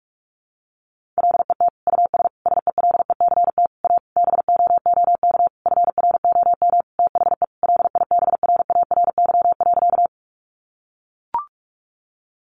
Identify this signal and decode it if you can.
Morse code, heard as a keyed tone, about 44 wpm, 700 Hz. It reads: LET US SELECT A BOOK FROM THE LIBRARY.
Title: Sentences from Top 1300 Words 45wpm